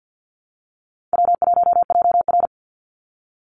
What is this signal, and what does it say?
Morse code, heard as a keyed tone, about 50 wpm, 700 Hz. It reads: W1JR
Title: Call Signs - 50wpm